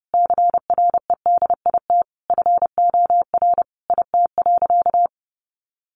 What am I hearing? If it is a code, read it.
Morse code, heard as a keyed tone, about 30 wpm, 700 Hz. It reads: CREDIT FOR IT.
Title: Sentences from Top 1000 Words 30wpm